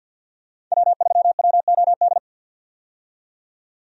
Morse code, heard as a keyed tone, about 52 wpm, 700 Hz. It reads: W3WOD
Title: USA Call Signs - 5 Characters - 50wpm